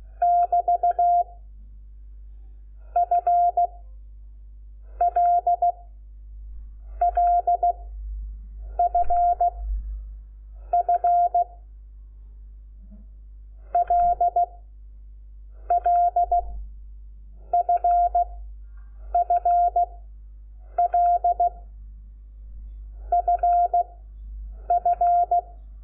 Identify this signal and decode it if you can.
Morse code, heard as a keyed tone, about 16 wpm, 700 Hz. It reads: <BT> F L L F F L L F F L F F
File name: Lecciones / CLASE Nº5